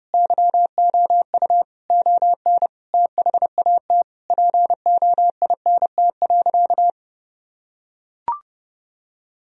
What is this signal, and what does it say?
Morse code, heard as a keyed tone, about 30 wpm, 700 Hz. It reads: YOU ON THAT POINT.
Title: Sentences from Top 1000 Words - Review Part 1 - 30wpm